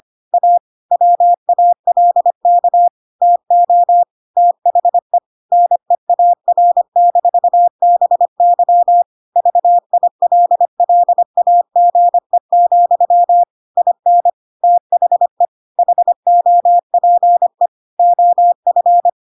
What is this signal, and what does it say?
Morse code, heard as a keyed tone, about 25 wpm, 700 Hz. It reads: A WALK TO THE NEAR-BY VILLAGE, IN THE HOPE OF
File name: Pinocchio / Pinocchio ch5 25 25 ik1hge